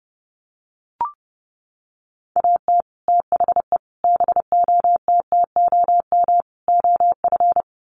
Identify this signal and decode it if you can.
Morse code, heard as a keyed tone, about 30 wpm, 700 Hz. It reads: AT THE BOTTOM OF
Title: Sets of 4 Words 30wpm